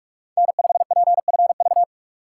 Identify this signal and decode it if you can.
Morse code, heard as a keyed tone, about 46 wpm, 700 Hz. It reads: N5PFV
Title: USA Call Signs - 5 Characters - 3x Character Spacing - 45wpm